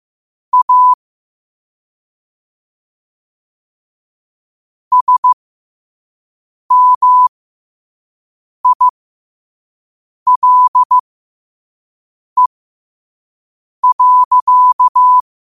Morse code, heard as a keyed tone, about 15 wpm, spaced as for 5 wpm, 1000 Hz. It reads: A SMILE.
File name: qotdfiles / 5wpm